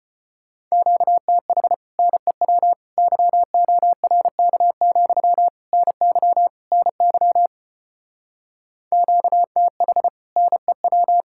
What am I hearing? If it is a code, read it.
Morse code, heard as a keyed tone, about 33 wpm, 700 Hz. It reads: QTH NEW YORK, NY NY QTH NEW